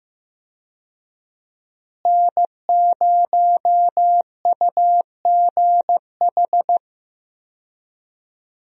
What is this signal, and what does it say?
Morse code, heard as a keyed tone, about 16 wpm, 700 Hz. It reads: N0UGH
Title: USA Call Signs - 5 Characters - Repeated 3x - 15wpm